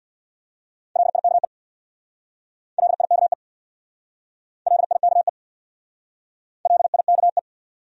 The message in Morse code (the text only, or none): LIKE LIKE LIKE LIKE